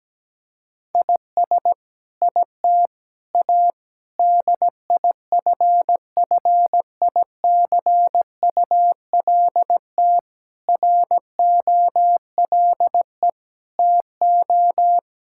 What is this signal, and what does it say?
Morse code, heard as a keyed tone, about 17 wpm, 700 Hz. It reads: IS IT A DIFFICULT ROLE TO
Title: Sentences from Top 1000 Words - Review Part 6 - 17wpm